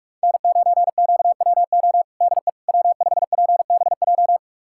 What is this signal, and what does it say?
Morse code, heard as a keyed tone, about 46 wpm, 700 Hz. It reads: N9QWO DE W5PBJ
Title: USA Callsign DE Callsign - 5 Characters - 45wpm